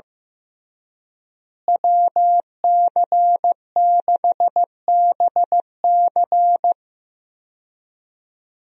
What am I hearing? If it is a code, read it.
Morse code, heard as a keyed tone, about 15 wpm, 700 Hz. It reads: WC6BC
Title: USA Call Signs - 5 Characters - Repeated 3x - 15wpm